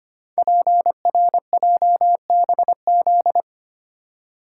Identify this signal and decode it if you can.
Morse code, heard as a keyed tone, about 25 wpm, 700 Hz. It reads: PRJBZ
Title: Code Groups - 5 Letters - 25wpm